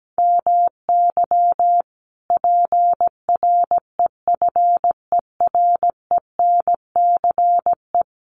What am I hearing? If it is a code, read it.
Morse code, heard as a keyed tone, about 17 wpm, 700 Hz. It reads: MY PREFERENCE
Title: Sentences from Top 1500 Words 17wpm